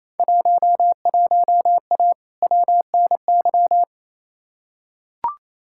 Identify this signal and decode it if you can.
Morse code, heard as a keyed tone, about 28 wpm, 700 Hz. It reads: 11A WNY
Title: Field Day Exchanges Encore - 28wpm